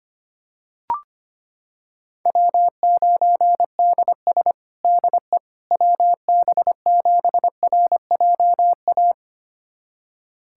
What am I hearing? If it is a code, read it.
Morse code, heard as a keyed tone, about 25 wpm, 700 Hz. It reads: W9DS DE WB7RJA
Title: USA Callsign DE Callsign 25wpm